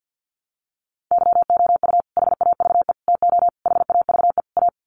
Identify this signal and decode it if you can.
Morse code, heard as a keyed tone, about 50 wpm, 700 Hz. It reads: YOU HAVE TO HAVE A
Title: Sets of 5 Words Encore 50wpm